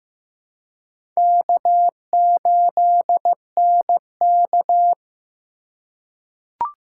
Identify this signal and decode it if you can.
Morse code, heard as a keyed tone, about 15 wpm, 700 Hz. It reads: K8NK
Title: USA Call Signs - 4x Character Spacing - 15wpm